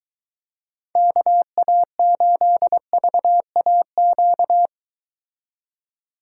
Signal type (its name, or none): Morse code